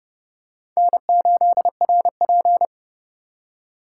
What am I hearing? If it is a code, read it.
Morse code, heard as a keyed tone, about 30 wpm, 700 Hz. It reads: N8RP